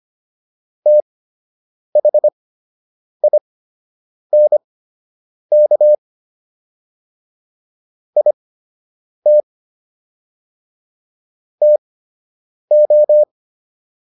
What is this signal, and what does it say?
Morse code, heard as a keyed tone, about 25 wpm, spaced as for 8 wpm, 600 Hz. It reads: THINK IT TO